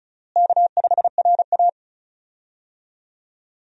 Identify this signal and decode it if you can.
Morse code, heard as a keyed tone, about 36 wpm, 700 Hz. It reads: K5RA
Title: USA Call Signs - 4 Characters - Repeated 2x - 35wpm